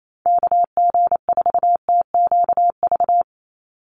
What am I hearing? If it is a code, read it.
Morse code, heard as a keyed tone, about 28 wpm, 700 Hz. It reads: KG4TQV
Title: USA Call Signs 28wpm